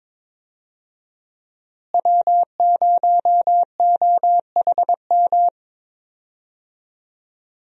Morse code, heard as a keyed tone, about 22 wpm, 700 Hz. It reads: W0OHM
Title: USA Call Signs - 5 Characters - Repeated 2x - 22wpm